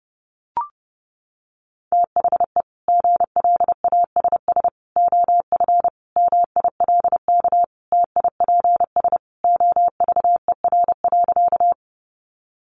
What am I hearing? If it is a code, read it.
Morse code, heard as a keyed tone, about 30 wpm, 700 Hz. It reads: THE GLASS OF MILK TIPS OVER.
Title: Additional Easy Phrases 30wpm